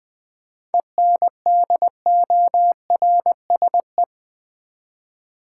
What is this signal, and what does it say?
Morse code, heard as a keyed tone, about 20 wpm, 700 Hz. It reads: ENDORSE
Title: 7 Letter Words - First 3 Letters Given - 20wpm